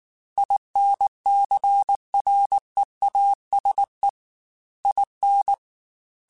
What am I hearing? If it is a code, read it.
Morse code, heard as a keyed tone, about 19 wpm, 800 Hz. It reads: INCREASE IN